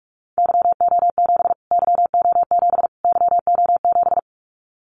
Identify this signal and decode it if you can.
Morse code, heard as a keyed tone, about 44 wpm, 700 Hz. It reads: YO7 YO7 YO7
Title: Callsign Prefixes for Top 25 DXCC - Repeated 3x - 45wpm